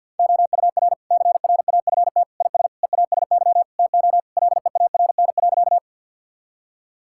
Morse code, heard as a keyed tone, about 52 wpm, 700 Hz. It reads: OUR CRAFT IS EASY TO LEARN.
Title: Sentences from Top 1000 Words - Review Part 6 - 50wpm